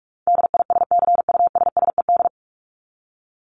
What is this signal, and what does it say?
Morse code, heard as a keyed tone, about 44 wpm, 700 Hz. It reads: DISCUSSED
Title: ED Suffix - Top 100 Words with Base Words 45wpm